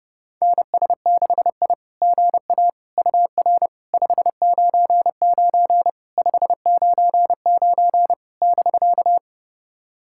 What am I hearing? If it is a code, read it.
Morse code, heard as a keyed tone, about 30 wpm, 700 Hz. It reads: NS6I GA UR 599 599 <BK>
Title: SOTA QSO Practice 30wpm